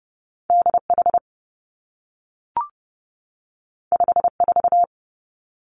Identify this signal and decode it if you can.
Morse code, heard as a keyed tone, about 30 wpm, 700 Hz. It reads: D H 5 4
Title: Two Challenging Letter-Number Combinations 30wpm